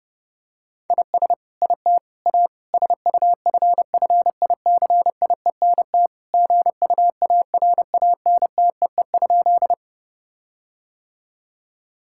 Morse code, heard as a keyed tone, about 30 wpm, 700 Hz. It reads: IS IT A SUFFICIENT GUARANTEE?
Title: Sentences from Top 1600 Words 30wpm